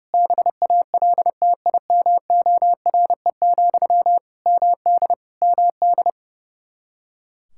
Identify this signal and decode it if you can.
Morse code, heard as a keyed tone, about 30 wpm, 700 Hz. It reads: BALTIMORE, MD MD
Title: Practice QSO - Exchange 1 - Beginning - 30wpm